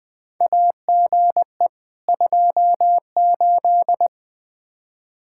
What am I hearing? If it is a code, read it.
Morse code, heard as a keyed tone, about 20 wpm, 700 Hz. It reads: AGE 28
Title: K1USN SST QSO Practice 20wpm